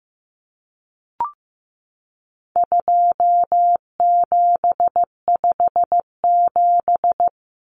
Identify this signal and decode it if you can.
Morse code, heard as a keyed tone, about 15 wpm, 700 Hz. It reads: 2757
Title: Common Cut Numbers - 4-Digit 15wpm